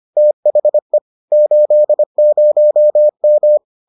Morse code, heard as a keyed tone, about 25 wpm, 600 Hz. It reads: THE 80M